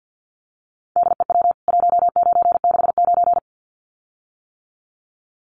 Morse code, heard as a keyed tone, about 50 wpm, 700 Hz. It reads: DEW 1969
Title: CWT Exchange 50wpm